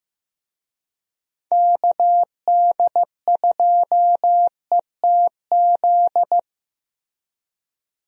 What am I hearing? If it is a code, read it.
Morse code, heard as a keyed tone, about 15 wpm, 700 Hz. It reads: KD2ETZ
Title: USA Call Signs - 6 Characters - Repeated 3x - 15wpm